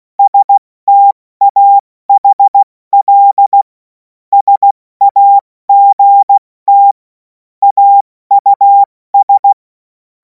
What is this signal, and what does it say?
Morse code, heard as a keyed tone, about 16 wpm, 800 Hz. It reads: STAHL SAGT AUS
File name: Morsekurs / Witze / Witz 10. Februar 2026 70 F80